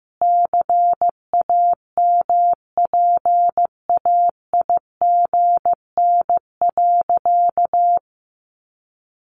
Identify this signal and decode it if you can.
Morse code, heard as a keyed tone, about 15 wpm, 700 Hz. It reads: CAMPAIGN.
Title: Sentences from Top 1200 Words 15wpm